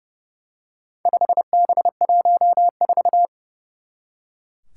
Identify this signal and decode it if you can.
Morse code, heard as a keyed tone, about 30 wpm, 700 Hz. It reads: 5B14
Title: Four Challenging Letter-Number Combinations 30wpm